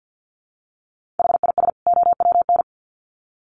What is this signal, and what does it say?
Morse code, heard as a keyed tone, about 50 wpm, 700 Hz. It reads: HIS OWN
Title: Sets of 2 Words 50wpm